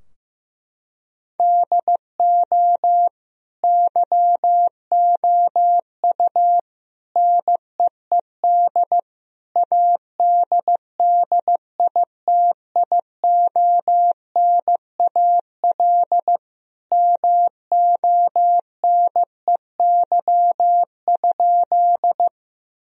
Morse code, heard as a keyed tone, about 15 wpm, 700 Hz. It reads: DO YOU NEED ADDITIONAL MONEY?
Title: Sentences from Top 1000 Words - Review Part 4 - 15wpm